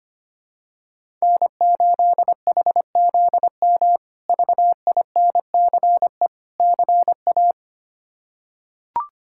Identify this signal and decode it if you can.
Morse code, heard as a keyed tone, about 25 wpm, 700 Hz. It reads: N8HZM VINCE CA
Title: K1USN SST QSO Practice - 25wpm